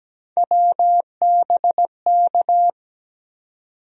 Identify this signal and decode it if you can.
Morse code, heard as a keyed tone, about 17 wpm, 700 Hz. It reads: WBK
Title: USA Call Sign Postfixes - 3 Letters 17wpm